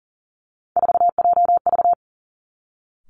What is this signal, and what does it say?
Morse code, heard as a keyed tone, about 38 wpm, 700 Hz. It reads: 4JV
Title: Three Challenging Letter-Number Combinations 40wpm